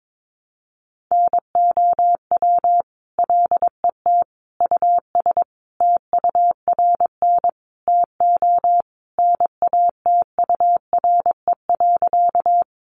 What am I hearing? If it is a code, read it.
Morse code, heard as a keyed tone, about 22 wpm, 700 Hz. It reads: NOW LET US TURN TO NATURE.
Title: Sentences from Top 600 Words 22wpm